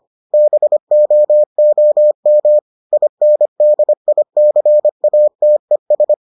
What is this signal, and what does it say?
Morse code, heard as a keyed tone, about 25 wpm, 600 Hz. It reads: BOOM INDICATES